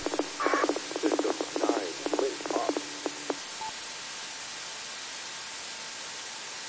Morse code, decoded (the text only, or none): none